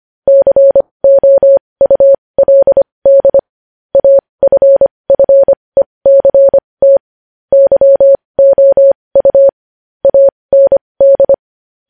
Morse code, heard as a keyed tone, about 25 wpm, spaced as for 20 wpm, 550 Hz. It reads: COULD AFFECT YOU AND
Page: News Headlines in Morse Code at 20 WPM